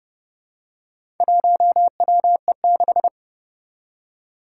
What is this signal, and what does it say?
Morse code, heard as a keyed tone, about 30 wpm, 700 Hz. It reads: 1WE6
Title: Four Challenging Letter-Number Combinations 30wpm